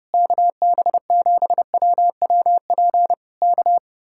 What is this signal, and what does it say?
Morse code, heard as a keyed tone, about 30 wpm, 700 Hz. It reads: KB7WWP K